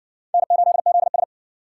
Morse code, heard as a keyed tone, about 54 wpm, 700 Hz. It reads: N8ZS